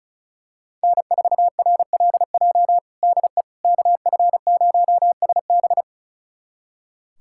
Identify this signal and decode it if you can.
Morse code, heard as a keyed tone, about 34 wpm, 700 Hz. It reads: N4RLJ DE KF0SB
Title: USA Callsign DE Callsign - 5 Characters - 35wpm